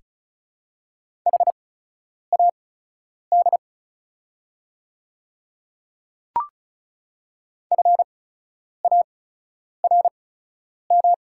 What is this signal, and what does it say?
Morse code, heard as a keyed tone, about 34 wpm, spaced as for 10 wpm, 690 Hz. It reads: HAD FARM
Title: 8x Character Spacing - Top 500 Words - 35wpm